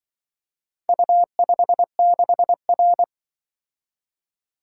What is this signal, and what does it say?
Morse code, heard as a keyed tone, about 24 wpm, 700 Hz. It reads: U56R
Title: Speed-Racing - Four Letter-Number Combinations - 20wpm